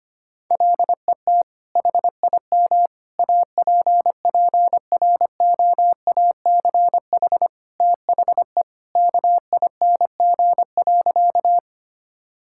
Morse code, heard as a keyed tone, about 25 wpm, 700 Hz. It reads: LET HIM APPROACH THE KING.
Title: Sentences from Top 1000 Words - Review Part 6 - 25wpm